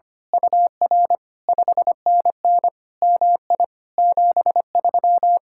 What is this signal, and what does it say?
Morse code, heard as a keyed tone, about 24 wpm, 700 Hz. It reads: UR 5NN MI 73
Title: Speed-Racing Common QSO Phrases - 25wpm